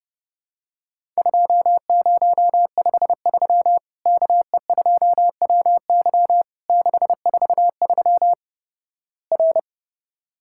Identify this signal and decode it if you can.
Morse code, heard as a keyed tone, about 30 wpm, 700 Hz. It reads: 2053 KE2WY 643 R
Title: Worked All Europe DX Contest - QTC Exchange - 30wpm